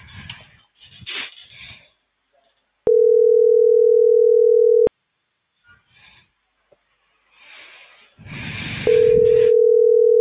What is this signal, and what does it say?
Ringback tone (North America), 440 + 480 Hz.